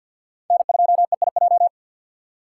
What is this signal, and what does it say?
Morse code, heard as a keyed tone, about 47 wpm, 700 Hz. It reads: N2EIJ
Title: USA Call Signs - 5 Characters - 4x Character Spacing - 50wpm